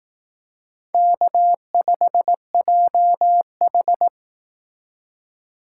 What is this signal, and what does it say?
Morse code, heard as a keyed tone, about 18 wpm, 700 Hz. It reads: K5JH
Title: Speed Racing - USA 1x2 Call Signs 15wpm